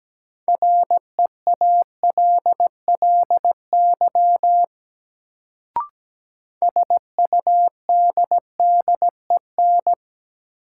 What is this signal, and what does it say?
Morse code, heard as a keyed tone, about 17 wpm, 700 Hz. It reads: REALLY SUDDEN
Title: Sets of 4 Words - New Word Drill - 17wpm